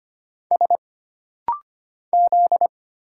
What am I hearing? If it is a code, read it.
Morse code, heard as a keyed tone, about 25 wpm, 700 Hz. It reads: S Z
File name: Single Character (Letter:Number:Punctuation:Prosign) - Rapid-Fire - 25wpm